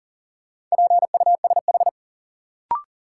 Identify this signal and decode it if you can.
Morse code, heard as a keyed tone, about 37 wpm, 700 Hz. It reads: PUSH
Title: Sentences from Top 1300 Words - New Word Drill - 40wpm